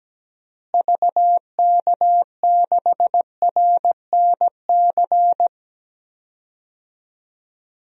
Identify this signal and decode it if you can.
Morse code, heard as a keyed tone, about 17 wpm, 700 Hz. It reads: VK6RNC
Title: DXCC-100 Call Signs - 6 Characters - Repeated 3x - 17wpm